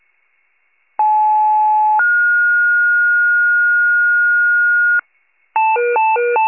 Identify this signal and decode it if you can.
Two-tone page, heard as a one-second tone, then a three-second tone, about 860 then 1430 Hz, with noise. It is heard from 1.0 to 5.0 s.